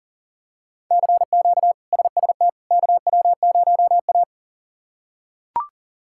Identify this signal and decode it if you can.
Morse code, heard as a keyed tone, about 40 wpm, 700 Hz. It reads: CQ SST KW0A
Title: K1USN SST QSO Practice - 40wpm